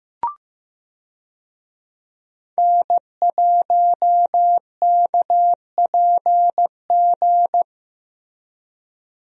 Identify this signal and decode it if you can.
Morse code, heard as a keyed tone, about 15 wpm, 700 Hz. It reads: N1KPG
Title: USA Call Signs - 5 Characters - Repeated 2x - 15wpm